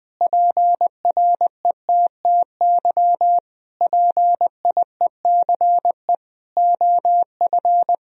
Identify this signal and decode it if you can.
Morse code, heard as a keyed tone, about 21 wpm, 700 Hz. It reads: PRETTY PIECE OF
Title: Sentences from Top 1000 Words - Review Part 2 - 20wpm